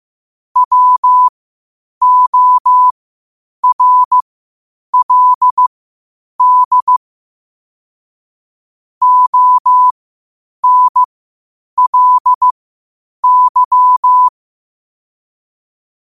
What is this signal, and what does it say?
Morse code, heard as a keyed tone, about 15 wpm, spaced as for 8 wpm, 1000 Hz. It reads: WORLD ONLY
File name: qotdfiles / 8wpm